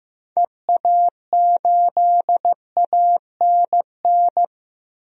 Morse code, heard as a keyed tone, about 15 wpm, 700 Hz. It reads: EA8ANN
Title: DXCC-100 Call Signs - 6 Characters - Repeated 2x - 15wpm